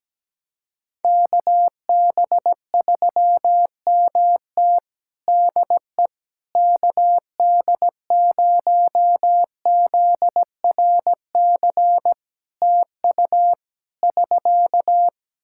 Morse code, heard as a keyed tone, about 17 wpm, 700 Hz. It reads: KB3MT DE KD0ZRC TU <SK>
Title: Practice QSO - Exchange 4 - Ending - 17wpm